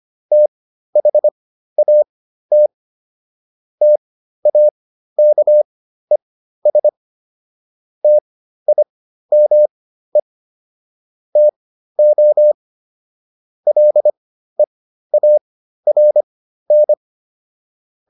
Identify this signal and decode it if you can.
Morse code, heard as a keyed tone, about 25 wpm, spaced as for 13 wpm, 600 Hz. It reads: THAT TAKES TIME TO LEARN